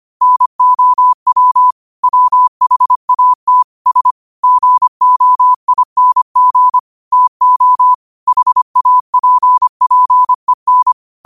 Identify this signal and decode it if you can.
Morse code, heard as a keyed tone, about 25 wpm, 1000 Hz. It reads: NOW WHAT S GOING TO HAPPEN